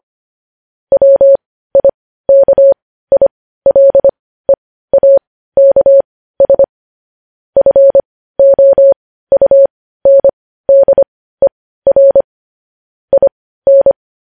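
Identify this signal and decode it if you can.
Morse code, heard as a keyed tone, about 25 wpm, spaced as for 15 wpm, 550 Hz. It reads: WIKILEAKS FOUNDER IN
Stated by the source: Morse code news headlines at 15 wpm.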